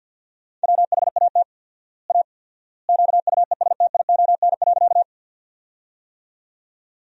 Morse code, heard as a keyed tone, about 52 wpm, 700 Hz. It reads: WHAT A QUESTION.
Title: 4x Word-Spacing - Sentences from Top 400 Words - 50wpm